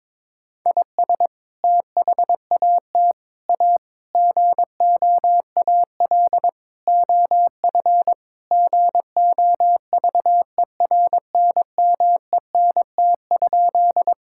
Morse code, heard as a keyed tone, about 22 wpm, 700 Hz. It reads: IS THAT A GOAL OF GOVERNMENT?
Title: Sentences from Top 1000 Words - Review Part 5 - 22wpm